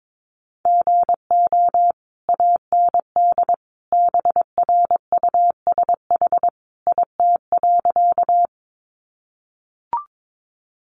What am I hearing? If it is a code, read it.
Morse code, heard as a keyed tone, about 22 wpm, 700 Hz. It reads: GO AND BRUSH IT.
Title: Sentences from Top 1100 Words 22wpm